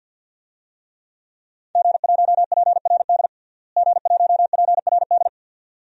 Morse code, heard as a keyed tone, about 46 wpm, 700 Hz. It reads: G1PRD G1PRD
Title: DXCC-100 Call Signs - 5 Characters - Repeated 2x - 50wpm